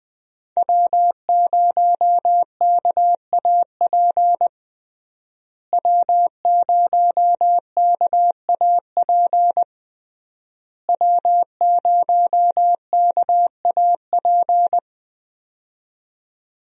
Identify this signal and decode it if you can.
Morse code, heard as a keyed tone, about 21 wpm, 700 Hz. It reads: W0KAP W0KAP W0KAP
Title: USA Call Signs - 5 Characters - Repeated 3x - 20wpm